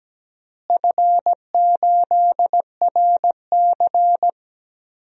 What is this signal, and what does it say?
Morse code, heard as a keyed tone, about 17 wpm, 700 Hz. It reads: F8RC
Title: DXCC-100 Call Signs - 4 Characters - Repeated 2x - 17wpm